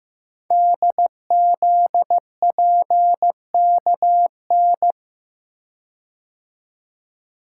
Morse code, heard as a keyed tone, about 15 wpm, 700 Hz. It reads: DZPKN
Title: Code Groups - 5 Letters - 15wpm